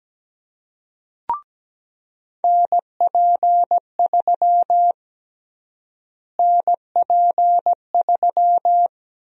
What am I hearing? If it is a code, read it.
Morse code, heard as a keyed tone, about 17 wpm, 700 Hz. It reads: NP3 NP3
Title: USA Call Sign Prefixes - Repeated 2x - 17wpm